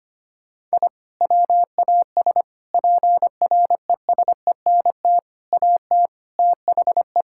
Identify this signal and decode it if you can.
Morse code, heard as a keyed tone, about 25 wpm, 700 Hz. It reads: I WAS PRESENT AT THE
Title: Sentences from Top 1000 Words - Review Part 5 - 25wpm